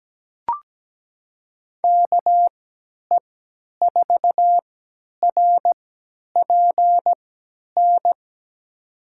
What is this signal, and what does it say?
Morse code, heard as a keyed tone, about 17 wpm, spaced as for 10 wpm, 700 Hz. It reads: KE4RPN
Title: USA Call Signs - 3x Character Spacing - 17wpm